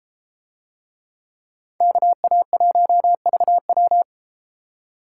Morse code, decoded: KA1VW

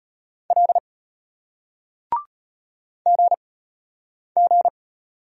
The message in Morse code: L GG

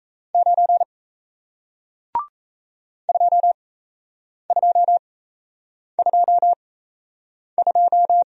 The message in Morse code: 9 2222

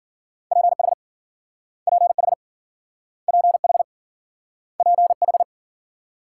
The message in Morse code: PH PH PH PH

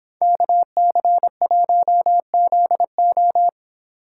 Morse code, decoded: KC1ZO

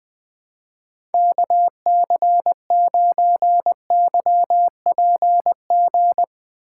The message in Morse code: KC9YPG

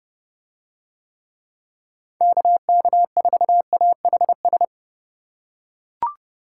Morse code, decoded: KK4AHS

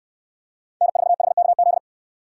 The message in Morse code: N4HKZ

